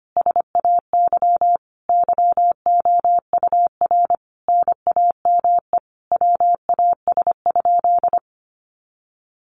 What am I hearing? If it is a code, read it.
Morse code, heard as a keyed tone, about 25 wpm, 700 Hz. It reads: SAY YOUR NAME WAS?
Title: Sentences from Top 200 Words 25wpm